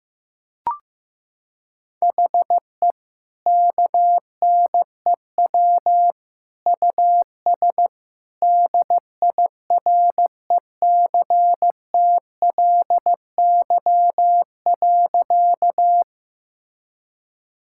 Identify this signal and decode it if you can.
Morse code, heard as a keyed tone, about 16 wpm, 700 Hz. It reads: HE KNEW US DIRECTLY.
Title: Sentences from Top 1000 Words - Review Part 5 - 15wpm